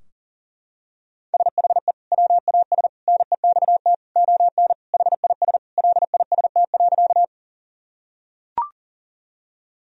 Morse code, heard as a keyed tone, about 40 wpm, 700 Hz. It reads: SHE WAS NEXT ON HIS LIST.
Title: Sentences from Top 1000 Words - Review Part 2 - 40wpm